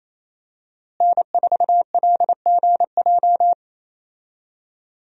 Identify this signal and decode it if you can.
Morse code, heard as a keyed tone, about 27 wpm, 700 Hz. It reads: N4LGJ
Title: USA Call Signs - 5 Characters - Repeated 3x - 28wpm